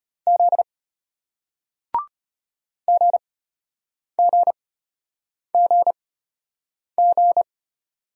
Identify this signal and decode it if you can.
Morse code, heard as a keyed tone, about 33 wpm, spaced as for 8 wpm, 700 Hz. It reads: Z GGGG